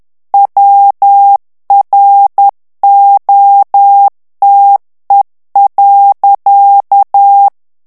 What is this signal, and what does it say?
Morse code, heard as a keyed tone, about 10 wpm, 800 Hz. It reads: WROTE.
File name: ANGLAIS texte clair 3 vitesse 10 mots minute